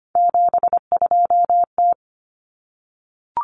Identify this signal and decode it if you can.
Morse code, heard as a keyed tone, about 25 wpm, 700 Hz. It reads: 72T
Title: Common Cut Numbers - 3-Digit 25wpm